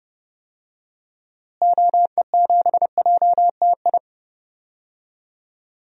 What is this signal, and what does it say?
Morse code, heard as a keyed tone, about 30 wpm, 700 Hz. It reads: OE7JTI
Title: DXCC-100 Call Signs - 6 Characters - Repeated 3x - 30wpm